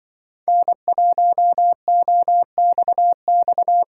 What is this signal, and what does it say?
Morse code, heard as a keyed tone, about 24 wpm, 700 Hz. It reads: N1OXX
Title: Speed Racing - USA 1x3 Call Signs 20wpm